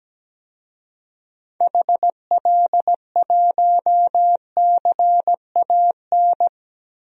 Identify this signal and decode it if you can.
Morse code, heard as a keyed tone, about 17 wpm, 700 Hz. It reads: HL1CAN
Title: DXCC-100 Call Signs - 6 Characters - 17wpm